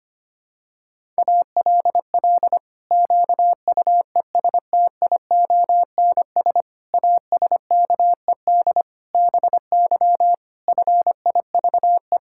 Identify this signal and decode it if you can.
Morse code, heard as a keyed tone, about 25 wpm, 700 Hz. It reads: ALL QUESTIONS ASKED BY FIVE
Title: Pangram Challenge v2 - 25wpm